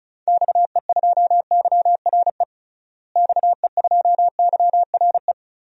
Morse code, heard as a keyed tone, about 36 wpm, 700 Hz. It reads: XE2YRE XE2YRE
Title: DXCC-100 Call Signs - 6 Characters - Repeated 2x - 35wpm